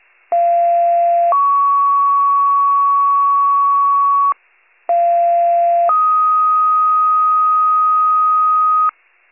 Two-tone page, heard as a one-second tone, then a three-second tone, about 690 then 1120 Hz, about 690 then 1250 Hz. It is heard from 0.3 to 8.9 s.